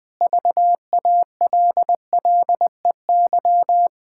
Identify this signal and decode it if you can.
Morse code, heard as a keyed tone, about 20 wpm, 700 Hz. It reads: VALLEY
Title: Pangram Challenge v2 - 20wpm